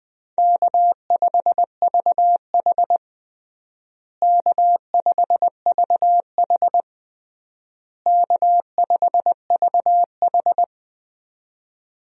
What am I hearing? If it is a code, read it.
Morse code, heard as a keyed tone, about 20 wpm, 700 Hz. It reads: K5VH K5VH K5VH
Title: USA Call Signs - 4 Characters - Repeated 3x - 20wpm